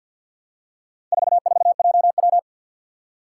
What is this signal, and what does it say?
Morse code, heard as a keyed tone, about 52 wpm, 700 Hz. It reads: 44JW
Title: Four Challenging Letter-Number Combinations 50wpm